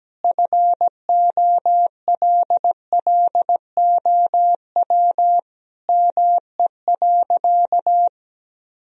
Morse code, heard as a keyed tone, about 17 wpm, 700 Hz. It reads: FOLLOW ME.
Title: Sentences from Top 1000 Words - Review Part 2 - 17wpm